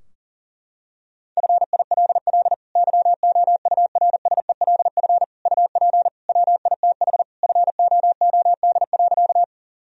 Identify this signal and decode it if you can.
Morse code, heard as a keyed tone, about 40 wpm, 700 Hz. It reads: FILL YOURSELF UP WITH FOOD.
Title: Sentences from Top 1000 Words - Review Part 3 - 40wpm